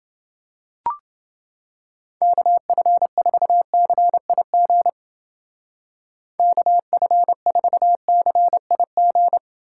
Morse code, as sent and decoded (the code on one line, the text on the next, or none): -.- ..-. ....- -.-. .. --. / -.- ..-. ....- -.-. .. --.
KF4CIG KF4CIG